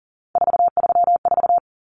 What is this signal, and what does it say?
Morse code, heard as a keyed tone, about 40 wpm, 700 Hz. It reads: 434
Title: Common Cut Numbers - 3-Digit 40wpm